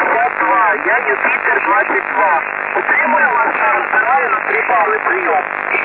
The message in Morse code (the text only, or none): none